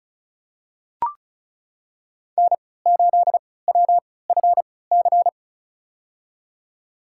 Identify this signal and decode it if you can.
Morse code, heard as a keyed tone, about 34 wpm, spaced as for 20 wpm, 700 Hz. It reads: N8WFC
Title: USA Call Signs - 5 Characters - 3x Character Spacing - 35wpm